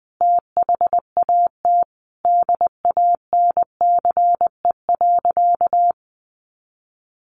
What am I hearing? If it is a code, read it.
Morse code, heard as a keyed tone, about 20 wpm, 700 Hz. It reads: THAT DANCE.
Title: Easy Phrases 20wpm